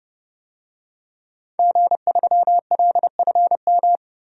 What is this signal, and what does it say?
Morse code, heard as a keyed tone, about 32 wpm, 700 Hz. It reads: G3LFM
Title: DXCC-100 Call Signs - 5 Characters - Repeated 3x - 30wpm